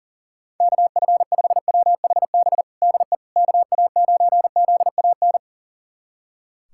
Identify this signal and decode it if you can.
Morse code, heard as a keyed tone, about 40 wpm, 700 Hz. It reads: KF5WHB DE KA9ZAN
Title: USA Callsign DE Callsign - 6 Characters - 40wpm